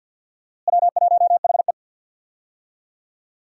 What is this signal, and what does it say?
Morse code, heard as a keyed tone, about 47 wpm, 700 Hz. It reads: W1HE
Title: USA Call Signs - 4 Characters - Repeated 3x - 50wpm